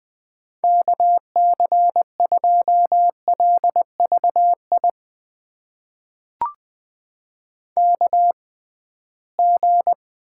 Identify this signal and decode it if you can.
Morse code, heard as a keyed tone, about 20 wpm, spaced as for 15 wpm, 700 Hz. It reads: KC2LVI KG
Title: USA Call Signs - 6 Characters - 6x Character Spacing - 20wpm